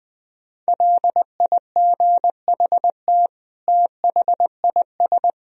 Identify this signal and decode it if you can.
Morse code, heard as a keyed tone, about 20 wpm, 700 Hz. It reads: LIGHT THIS
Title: Sentences from Top 1000 Words - Review Part 1 - 20wpm